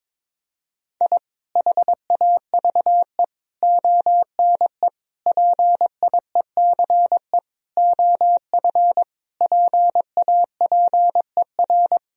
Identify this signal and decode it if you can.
Morse code, heard as a keyed tone, about 22 wpm, 700 Hz. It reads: I HAVE ONE PIECE OF PAPER
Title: Sentences from Top 1000 Words - Review Part 2 - 22wpm